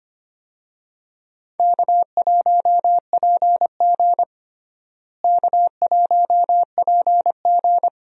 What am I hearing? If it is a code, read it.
Morse code, heard as a keyed tone, about 25 wpm, 700 Hz. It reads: K1PG K1PG
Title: USA Call Signs - 4 Characters - Repeated 3x - 25wpm